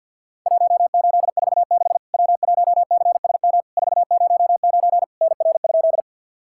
Morse code, heard as a keyed tone, about 50 wpm, 700 Hz. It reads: 1846 W1CSM 409 NR?
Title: Worked All Europe DX Contest - QTC Exchange with Repeats - 50wpm